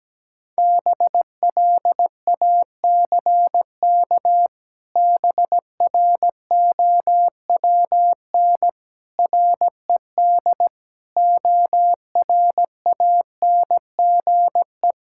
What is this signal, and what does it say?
Morse code, heard as a keyed tone, about 17 wpm, 700 Hz. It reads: BLACK BROWN RED ORANGE
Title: Pangram Challenge v2 - 17wpm